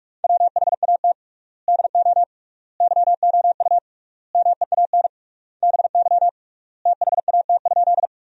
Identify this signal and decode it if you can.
Morse code, heard as a keyed tone, about 41 wpm, 700 Hz. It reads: WHAT DO YOU MEAN BY THAT?
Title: 3x Word-Spacing - Sentences from Top 200 Words - 45wpm